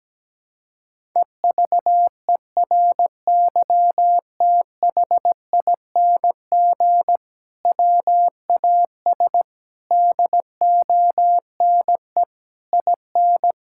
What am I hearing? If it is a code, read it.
Morse code, heard as a keyed tone, about 17 wpm, 700 Hz. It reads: EVERYTHING WAS DONE IN
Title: Sentences from Top 1600 Words 17wpm